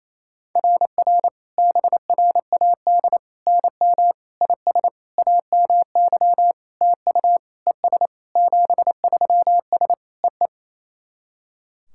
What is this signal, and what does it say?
Morse code, heard as a keyed tone, about 28 wpm, 690 Hz. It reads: RR BRAD NM IS AMY TU ES 73S EE
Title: SOTA QSO Practice 28wpm